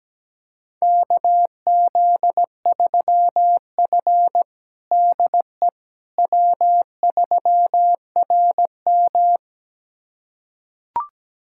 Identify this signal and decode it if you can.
Morse code, heard as a keyed tone, about 17 wpm, 700 Hz. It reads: KZ3F DE W3RM
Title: USA Callsign DE Callsign - 4 Characters - 17wpm